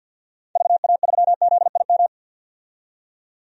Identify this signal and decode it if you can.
Morse code, heard as a keyed tone, about 46 wpm, 700 Hz. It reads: VA3ZIM
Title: DXCC-100 Call Signs - 6 Characters - 50wpm